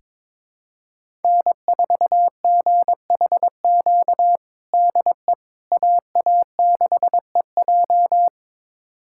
Morse code, heard as a keyed tone, about 22 wpm, 700 Hz. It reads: N4GHQ DE AA6EJ
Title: USA Callsign DE Callsign - 5 Characters - 22wpm